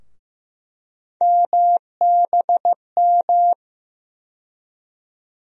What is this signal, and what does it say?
Morse code, heard as a keyed tone, about 15 wpm, 700 Hz. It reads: MBM
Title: USA Call Sign Postfixes - 3 Letters 15wpm